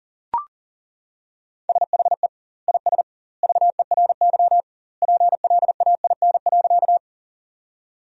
Morse code, heard as a keyed tone, about 38 wpm, 700 Hz. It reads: SHE IS VERY PLAIN.
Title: 2x Word-Spacing - Sentences from Top 400 Words - 40wpm